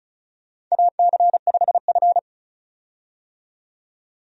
Morse code, heard as a keyed tone, about 36 wpm, 700 Hz. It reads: AC5F
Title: USA Call Signs - 4 Characters - 35wpm